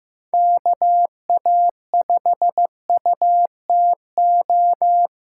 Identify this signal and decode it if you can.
Morse code, heard as a keyed tone, about 15 wpm, 700 Hz. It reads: KA5UTO
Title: USA Call Signs - 6x Character Spacing - 15wpm